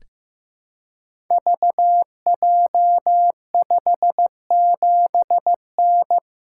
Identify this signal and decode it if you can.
Morse code, heard as a keyed tone, about 15 wpm, 700 Hz. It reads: VJ57N
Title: Code Groups - 5 Letter:Numbers - 15wpm